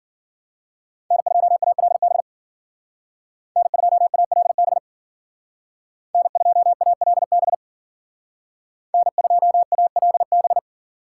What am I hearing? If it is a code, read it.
Morse code, heard as a keyed tone, about 52 wpm, 700 Hz. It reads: N2ALB N2ALB N2ALB N2ALB